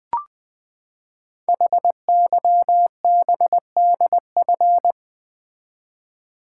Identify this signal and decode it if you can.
Morse code, heard as a keyed tone, about 20 wpm, 700 Hz. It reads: HYBDF
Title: Code Groups - 5 Letter:Numbers - 20wpm